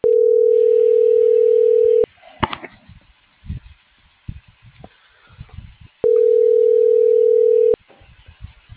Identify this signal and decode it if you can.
Ringback tone (North America), 440 + 480 Hz.